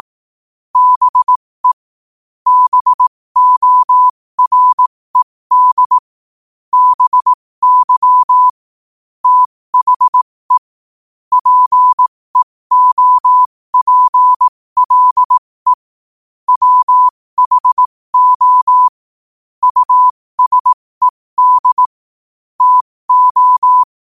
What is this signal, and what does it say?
Morse code, heard as a keyed tone, about 18 wpm, spaced as for 15 wpm, 1000 Hz. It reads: BE BORED BY THE PEOPLE WHO USED TO
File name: qotdfiles / 15wpm